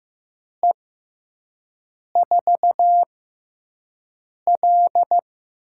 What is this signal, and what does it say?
Morse code, heard as a keyed tone, about 15 wpm, spaced as for 5 wpm, 700 Hz. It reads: E4L
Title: USA Call Signs - 6x Character Spacing - 15wpm